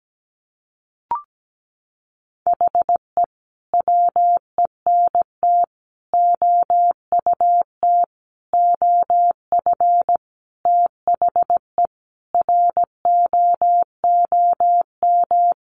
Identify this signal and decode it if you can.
Morse code, heard as a keyed tone, about 17 wpm, 700 Hz. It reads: HE WENT OUT OF THE ROOM
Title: Sentences from Top 1100 Words 17wpm